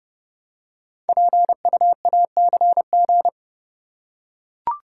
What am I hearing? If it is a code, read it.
Morse code, heard as a keyed tone, about 29 wpm, 700 Hz. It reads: PUACG
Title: Code Groups - 5 Letters - 30wpm